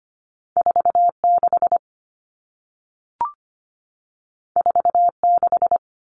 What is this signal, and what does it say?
Morse code, heard as a keyed tone, about 25 wpm, 700 Hz. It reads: 46 46
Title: All Cut Numbers - 2-Digit 25wpm